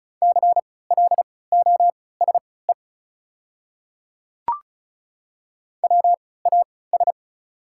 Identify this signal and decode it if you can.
Morse code, heard as a keyed tone, about 34 wpm, spaced as for 20 wpm, 700 Hz. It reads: CLOSE WAS
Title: 3x Character Spacing - Top 300 Words - 35wpm